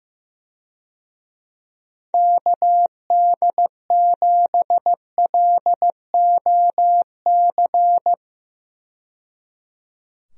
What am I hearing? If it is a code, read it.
Morse code, heard as a keyed tone, about 15 wpm, 700 Hz. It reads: KD7LOC
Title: USA Call Signs - 6 Characters - 15wpm